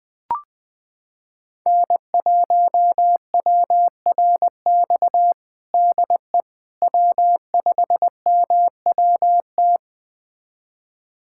Morse code, heard as a keyed tone, about 20 wpm, 700 Hz. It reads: N1WRX DE W5MWT
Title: USA Callsign DE Callsign - 5 Characters - 20wpm